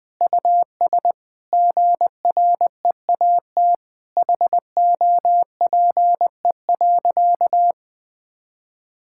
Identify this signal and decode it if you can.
Morse code, heard as a keyed tone, about 19 wpm, 700 Hz. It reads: US GREAT HOPE.
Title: Sentences from Top 1000 Words - Review Part 5 - 20wpm